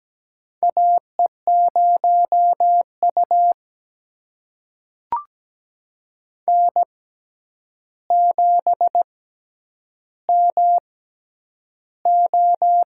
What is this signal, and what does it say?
Morse code, heard as a keyed tone, about 17 wpm, spaced as for 9 wpm, 700 Hz. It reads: AE0U N7MO
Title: USA Call Signs - 4 Characters - 6x Character Spacing - 17wpm